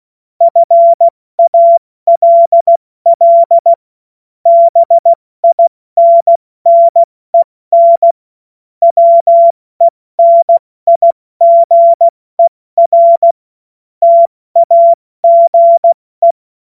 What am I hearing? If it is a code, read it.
Morse code, heard as a keyed tone, about 16 wpm, 680 Hz. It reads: FALL BINNEN WENIGER TAGE
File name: Morsekurs / Witze / Witz 30. August 2022 70 F80